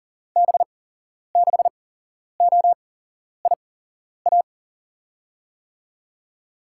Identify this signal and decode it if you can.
Morse code, heard as a keyed tone, about 39 wpm, 700 Hz. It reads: B 6 O I A
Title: USA Call Signs - 6 Characters - 8x Character Spacing - 40wpm